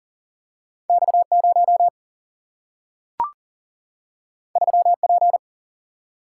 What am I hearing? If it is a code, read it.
Morse code, heard as a keyed tone, about 40 wpm, 700 Hz. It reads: X 0 3 P
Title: Single-Letter Single-Number 40wpm